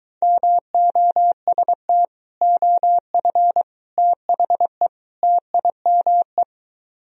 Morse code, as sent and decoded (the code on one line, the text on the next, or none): -- --- ... - / --- ..-. / - .... / - .. -- .
MOST OF THE TIME